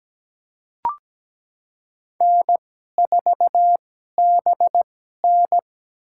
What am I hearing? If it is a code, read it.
Morse code, heard as a keyed tone, about 17 wpm, spaced as for 12 wpm, 700 Hz. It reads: N4BN
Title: USA Call Signs - 2x Character Spacing - 17wpm